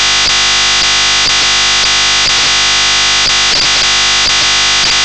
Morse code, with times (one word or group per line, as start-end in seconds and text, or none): none